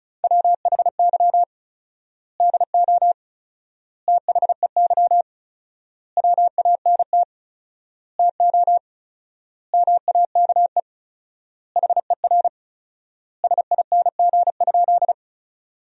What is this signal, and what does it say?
Morse code, heard as a keyed tone, about 36 wpm, 700 Hz. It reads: WHY DO THEY WANT TO MAKE HER SING?
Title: 4x Word-Spacing - Sentences from Top 400 Words - 35wpm